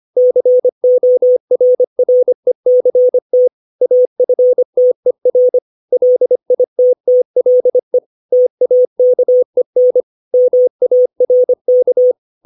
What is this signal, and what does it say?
Morse code, heard as a keyed tone, about 25 wpm, 500 Hz. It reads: CORRECT AFTER LITTLE TAKEN MARK
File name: 632 words / 632 Words 25WPM